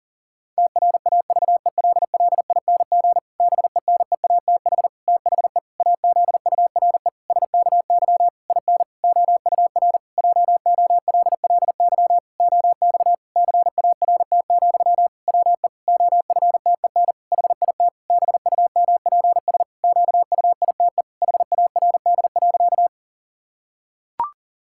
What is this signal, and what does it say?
Morse code, heard as a keyed tone, about 40 wpm, 700 Hz. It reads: TRAVELLING BENEATH THE AZURE SKY IN OUR JOLLY OX CART, WE OFTEN HIT BUMPS QUITE HARD.
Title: Pangram Challenge v2 - 40wpm